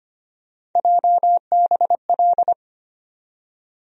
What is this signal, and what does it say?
Morse code, heard as a keyed tone, about 25 wpm, 700 Hz. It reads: JBL
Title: USA Call Sign Postfixes - 3 Letters 25wpm